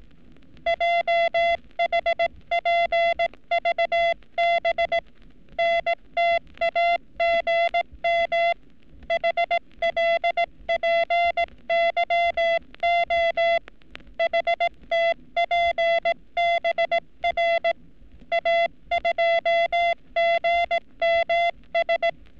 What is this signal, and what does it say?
Morse code, heard as a keyed tone, about 18 wpm, 680 Hz. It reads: JHPVB NTAGM HLPYO HTPBR A2GMS
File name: training / NAVPERS 11224-11235 / NAVPERS 11233 RA RB